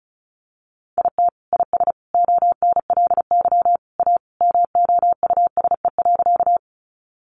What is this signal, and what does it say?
Morse code, heard as a keyed tone, about 34 wpm, 700 Hz. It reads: IT IS ONLY A MOUSE.
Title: Sentences from Top 700 Words 35wpm